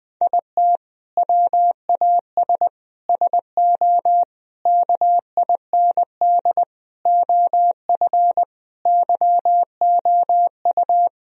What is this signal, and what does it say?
Morse code, heard as a keyed tone, about 20 wpm, 700 Hz. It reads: IT WAS SO KIND OF YOU
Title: Sentences from Top 1000 Words - Review Part 3 - 20wpm